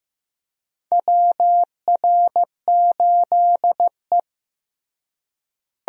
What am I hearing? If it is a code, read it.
Morse code, heard as a keyed tone, about 15 wpm, 700 Hz. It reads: WR8E
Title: USA Call Signs - 4 Characters - Repeated 3x - 15wpm